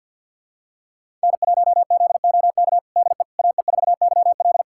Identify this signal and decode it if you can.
Morse code, heard as a keyed tone, about 47 wpm, 700 Hz. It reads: N1ZOK DE AE4YL
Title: USA Callsign DE Callsign - 5 Characters - 50wpm